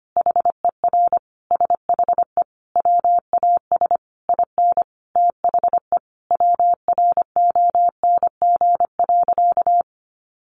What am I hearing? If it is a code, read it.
Morse code, heard as a keyed tone, about 25 wpm, 700 Hz. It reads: HER SHE WAS IN THE WRONG.
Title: Sentences from Top 1200 Words 25wpm